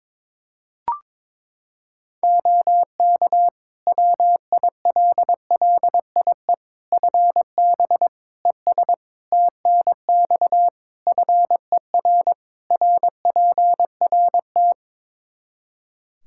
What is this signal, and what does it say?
Morse code, heard as a keyed tone, about 22 wpm, 700 Hz. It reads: OK WILLIE FB ES TNX FER RPRT
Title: Practice QSO - Exchange 2 - Info - 22wpm